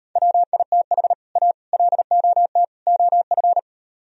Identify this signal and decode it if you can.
Morse code, heard as a keyed tone, about 38 wpm, 700 Hz. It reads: WITH A LOT OF